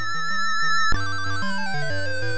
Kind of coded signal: none